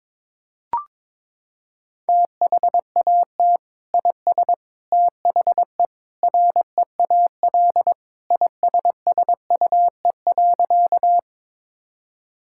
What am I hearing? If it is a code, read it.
Morse code, heard as a keyed tone, about 22 wpm, 700 Hz. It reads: THAT IS THE REAL ISSUE.
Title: Sentences from Top 1000 Words - Review Part 6 - 22wpm